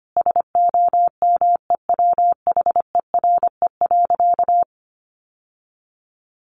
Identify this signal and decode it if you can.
Morse code, heard as a keyed tone, about 25 wpm, 700 Hz. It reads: SOMEWHERE.
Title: Sentences from Top 1400 Words 25wpm